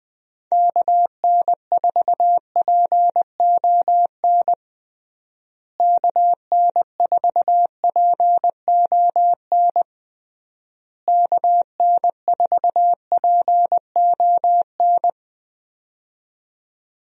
Morse code, heard as a keyed tone, about 20 wpm, 700 Hz. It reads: KN4PON KN4PON KN4PON
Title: USA Call Signs - 6 Characters - Repeated 3x - 20wpm